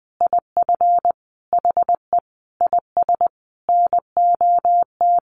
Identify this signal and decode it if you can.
Morse code, heard as a keyed tone, about 20 wpm, 700 Hz. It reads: IF HE IS NOT
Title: Sentences from Top 900 Words 20wpm